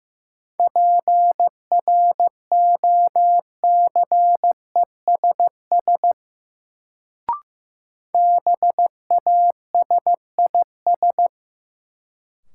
Morse code, heard as a keyed tone, about 15 wpm, 700 Hz. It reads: PROCESS BASIS
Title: Sets of 3 Words - New Word Drill - 15wpm